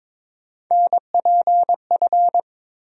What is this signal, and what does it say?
Morse code, heard as a keyed tone, about 22 wpm, 700 Hz. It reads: NPF
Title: USA Call Sign Postfixes - 3 Letters 22wpm